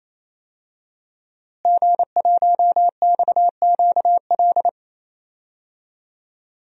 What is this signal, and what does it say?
Morse code, heard as a keyed tone, about 28 wpm, 700 Hz. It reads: G1XQL
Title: DXCC-100 Call Signs - 5 Characters - Repeated 3x - 28wpm